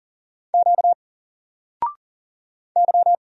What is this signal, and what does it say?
Morse code, heard as a keyed tone, about 41 wpm, 700 Hz. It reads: Q Y
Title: Single Letter-Number 40wpm